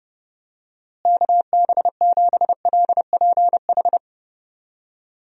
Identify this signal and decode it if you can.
Morse code, heard as a keyed tone, about 32 wpm, 700 Hz. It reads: KB7LPH
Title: USA Call Signs - 6 Characters - Repeated 3x - 30wpm